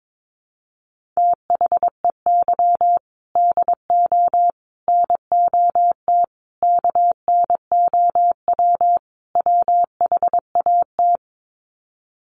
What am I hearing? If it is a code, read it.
Morse code, heard as a keyed tone, about 22 wpm, 700 Hz. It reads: THEY DO NOT KNOW WHAT
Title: Sets of 5 Words Encore 22wpm